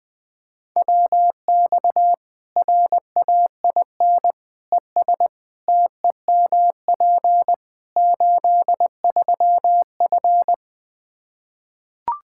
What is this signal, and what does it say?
Morse code, heard as a keyed tone, about 20 wpm, 700 Hz. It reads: WX RAIN ES TEMP 83F
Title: Practice QSO - Exchange 2 - Info - 20wpm